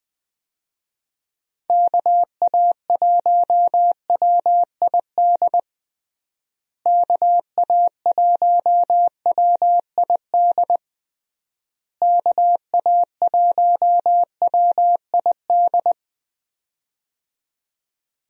Morse code, heard as a keyed tone, about 20 wpm, 700 Hz. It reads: KA1WID KA1WID KA1WID
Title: USA Call Signs - 6 Characters - Repeated 3x - 20wpm